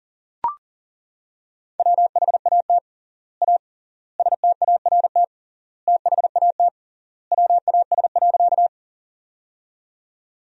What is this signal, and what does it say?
Morse code, heard as a keyed tone, about 40 wpm, 700 Hz. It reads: WHAT A START THAT WAS.
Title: 3x Word-Spacing - Sentences from Top 300 Words - 40wpm